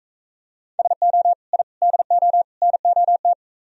Morse code, heard as a keyed tone, about 41 wpm, 700 Hz. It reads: SO I DO NOT